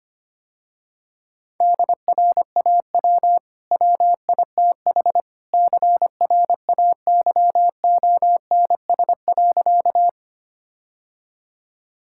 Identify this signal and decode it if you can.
Morse code, heard as a keyed tone, about 25 wpm, 700 Hz. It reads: DRAW WITH CRAYONS.
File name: Easy Sentences with Hints - 25wpm